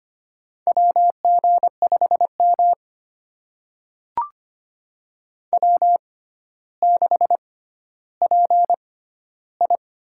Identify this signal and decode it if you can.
Morse code, heard as a keyed tone, about 25 wpm, spaced as for 13 wpm, 700 Hz. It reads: WG5M W6PI
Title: USA Call Signs - 4 Characters - 6x Character Spacing - 25wpm